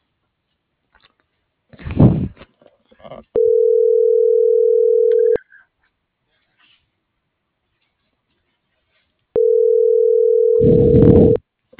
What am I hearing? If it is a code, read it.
Ringback tone (North America), 440 + 480 Hz.